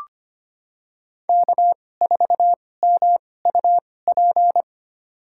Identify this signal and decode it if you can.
Morse code, heard as a keyed tone, about 25 wpm, spaced as for 18 wpm, 700 Hz. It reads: K4MUP
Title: USA Call Signs - 5 Characters - 2x Character Spacing - 25wpm